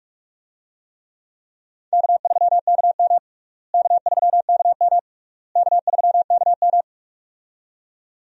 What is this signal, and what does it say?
Morse code, heard as a keyed tone, about 43 wpm, 700 Hz. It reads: K3KM K3KM K3KM
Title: USA Call Signs - 4 Characters - Repeated 3x - 45wpm